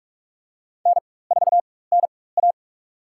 Morse code, heard as a keyed tone, about 46 wpm, spaced as for 21 wpm, 700 Hz. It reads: N4NA
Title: USA Call Signs - 4 Characters - 4x Character Spacing - 45wpm